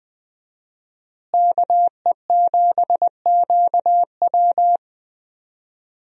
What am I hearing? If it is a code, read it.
Morse code, heard as a keyed tone, about 20 wpm, 700 Hz. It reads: KE7QW
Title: USA Call Signs - 5 Characters - Repeated 2x - 20wpm